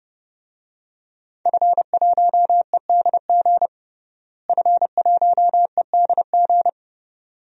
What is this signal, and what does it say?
Morse code, heard as a keyed tone, about 30 wpm, 700 Hz. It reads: F1EDG F1EDG
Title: DXCC-100 Call Signs - 5 Characters - Repeated 2x - 30wpm